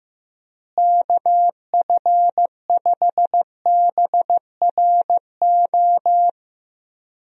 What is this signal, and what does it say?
Morse code, heard as a keyed tone, about 15 wpm, 700 Hz. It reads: KF5BRO
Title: USA Call Signs - 6 Characters - Repeated 3x - 15wpm